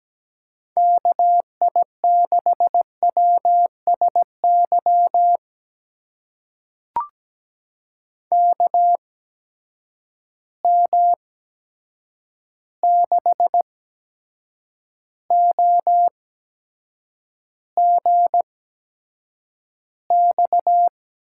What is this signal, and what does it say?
Morse code, heard as a keyed tone, about 17 wpm, spaced as for 7 wpm, 700 Hz. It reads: KI6WSY KM6OGX
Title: USA Call Signs - 6 Characters - 8x Character Spacing - 17wpm